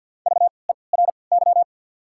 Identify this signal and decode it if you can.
Morse code, heard as a keyed tone, about 54 wpm, spaced as for 29 wpm, 690 Hz. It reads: VERY
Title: 3x Character Spacing - Top 200 Words - 50wpm